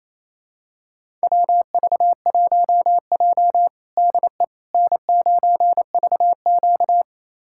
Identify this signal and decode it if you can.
Morse code, heard as a keyed tone, about 28 wpm, 700 Hz. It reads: WV1J DE N9VQ
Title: USA Callsign DE Callsign - 4 Characters - 28wpm